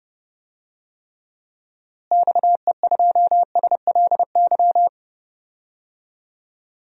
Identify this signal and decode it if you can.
Morse code, heard as a keyed tone, about 30 wpm, 700 Hz. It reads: XE2SLY
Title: DXCC-100 Call Signs - 6 Characters - Repeated 3x - 30wpm